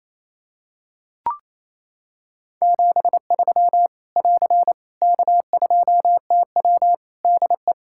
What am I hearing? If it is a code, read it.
Morse code, heard as a keyed tone, about 28 wpm, 700 Hz. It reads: 73 <AR> K2TW DE
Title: K1USN SST QSO Practice 28wpm